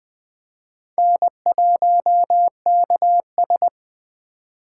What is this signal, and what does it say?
Morse code, heard as a keyed tone, about 21 wpm, 700 Hz. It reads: N1KS
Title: Speed Racing - USA 1x2 Call Signs 17wpm